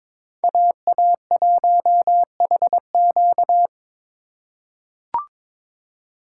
Morse code, heard as a keyed tone, about 22 wpm, 700 Hz. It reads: AA1HQ
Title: USA Call Signs - 5 Characters - 2x Character Spacing - 22wpm